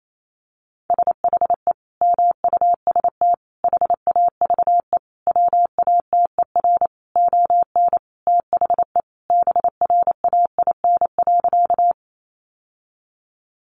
Morse code, heard as a keyed tone, about 28 wpm, 700 Hz. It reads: SHE MUST HAVE WATER ON THE BRAIN.
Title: Sentences from Top 1100 Words 28wpm